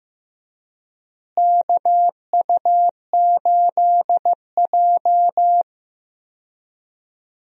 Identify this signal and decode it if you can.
Morse code, heard as a keyed tone, about 15 wpm, 700 Hz. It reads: KU8J
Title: USA Call Signs - 4 Characters - Repeated 2x - 15wpm